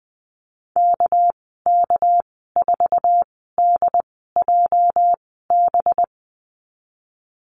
Morse code, heard as a keyed tone, about 20 wpm, spaced as for 14 wpm, 700 Hz. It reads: KK4DJB
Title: 213 Lesson 34g - Number 8 - Call Signs 20wpm